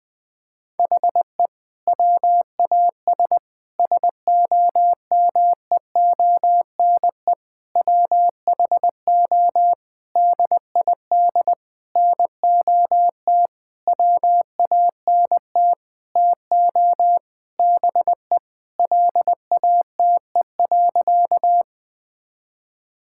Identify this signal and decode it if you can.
Morse code, heard as a keyed tone, about 21 wpm, 700 Hz. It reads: HE WAS SOMEONE WHO DID NOT WANT TO BE LATE.
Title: Sentences with 6-Word Phrases - 20wpm